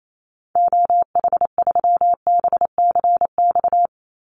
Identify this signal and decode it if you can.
Morse code, heard as a keyed tone, about 28 wpm, 700 Hz. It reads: OH3BCX
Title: Call Signs - 28wpm